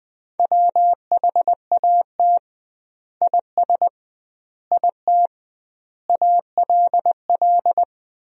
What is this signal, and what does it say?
Morse code, heard as a keyed tone, about 21 wpm, 700 Hz. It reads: WHAT IS IT ALL
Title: 2x Word-Spacing - Sentences from Top 100 Words - 20wpm